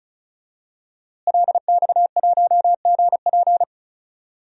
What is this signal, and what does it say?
Morse code, heard as a keyed tone, about 34 wpm, 700 Hz. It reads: LX1GP
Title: DXCC-100 Call Signs - 5 Characters - Repeated 2x - 35wpm